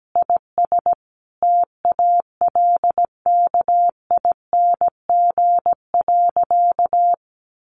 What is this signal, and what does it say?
Morse code, heard as a keyed tone, about 17 wpm, 700 Hz. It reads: IS TALKING.
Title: Additional Easy Phrases 17wpm